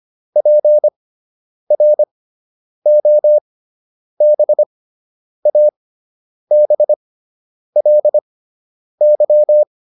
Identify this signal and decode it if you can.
Morse code, heard as a keyed tone, about 25 wpm, spaced as for 9 wpm, 600 Hz. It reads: PROBABLY